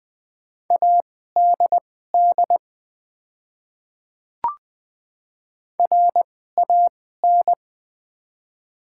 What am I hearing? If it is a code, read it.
Morse code, heard as a keyed tone, about 20 wpm, spaced as for 14 wpm, 690 Hz. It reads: ADD RAN
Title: 2x Character Spacing - Top 500 Words - 20wpm